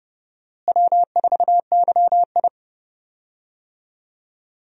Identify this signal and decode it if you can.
Morse code, heard as a keyed tone, about 30 wpm, 700 Hz. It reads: W4YI
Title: USA Call Signs - 4 Characters - Repeated 2x - 30wpm